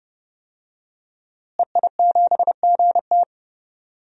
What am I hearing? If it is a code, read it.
Morse code, heard as a keyed tone, about 32 wpm, 700 Hz. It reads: EI7GT
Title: DXCC-100 Call Signs - 5 Characters - Repeated 3x - 30wpm